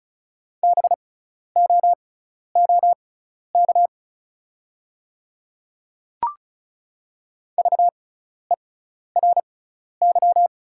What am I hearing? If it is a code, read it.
Morse code, heard as a keyed tone, about 36 wpm, spaced as for 12 wpm, 700 Hz. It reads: BOOK VERY
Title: 6x Character Spacing - Top 300 Words - 35wpm